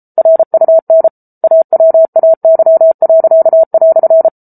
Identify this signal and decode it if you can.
Morse code, heard as a keyed tone, about 34 wpm, 650 Hz. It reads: RUN AWAY."